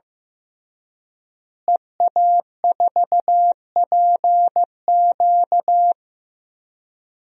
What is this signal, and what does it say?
Morse code, heard as a keyed tone, about 16 wpm, 700 Hz. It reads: EA4PQ
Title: DXCC-100 Call Signs - 5 Characters - Repeated 2x - 15wpm